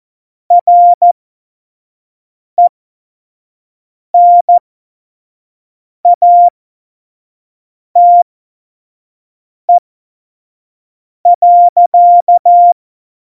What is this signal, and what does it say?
Morse code, heard as a keyed tone, about 14 wpm, spaced as for 5 wpm, 700 Hz. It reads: RENATE.
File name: Morsekurs / Witze / Witz 25. Juni 2019, 25 F70